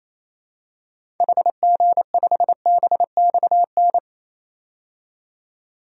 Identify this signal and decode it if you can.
Morse code, heard as a keyed tone, about 28 wpm, 700 Hz. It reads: HG5BXN
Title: DXCC-100 Call Signs - 6 Characters - 28wpm